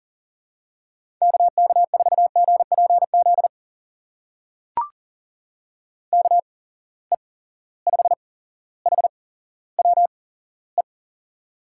Morse code, heard as a keyed tone, about 40 wpm, spaced as for 17 wpm, 700 Hz. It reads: KK4GPZ KE5HWE
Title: USA Call Signs - 6 Characters - 8x Character Spacing - 40wpm